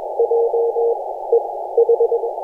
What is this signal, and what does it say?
Morse code, heard as a keyed tone, about 21 wpm, 470 Hz, with noise. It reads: J E H
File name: posts / 2016-jan-16-2256z-e51j-18-090mhz-cw-cqing